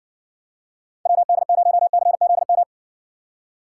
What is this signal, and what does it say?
Morse code, heard as a keyed tone, about 60 wpm, 700 Hz. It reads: WD0CZM